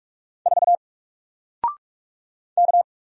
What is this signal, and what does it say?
Morse code, heard as a keyed tone, about 44 wpm, 700 Hz. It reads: 4 K
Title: Single Letter-Number Alt 45wpm